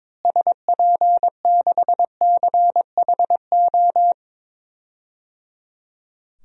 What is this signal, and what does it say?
Morse code, heard as a keyed tone, about 23 wpm, 700 Hz. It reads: SP6CHO
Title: DXCC-100 Call Signs - 6 Characters - Repeated 3x - 22wpm